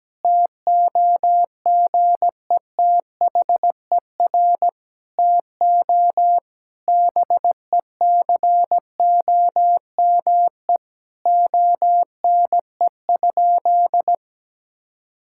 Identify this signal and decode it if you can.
Morse code, heard as a keyed tone, about 17 wpm, 700 Hz. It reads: TOGETHER TO BECOME ONE?
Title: Sentences from Top 1600 Words 17wpm